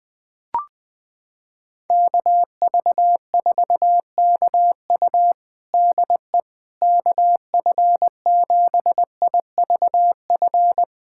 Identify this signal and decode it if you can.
Morse code, heard as a keyed tone, about 21 wpm, 700 Hz. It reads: KV4KU DE KF7IVF
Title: USA Callsign DE Callsign 20wpm